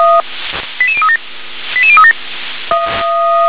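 Five-tone selective call (ZVEI), heard as quick runs of five tones, tones of about 70 ms. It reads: 88037 88037
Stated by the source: ZVEI five-tone call received by SDR.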